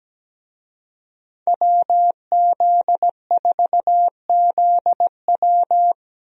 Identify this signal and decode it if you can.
Morse code, heard as a keyed tone, about 17 wpm, 700 Hz. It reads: WZ4ZW
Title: USA Call Signs - 5 Characters - 17wpm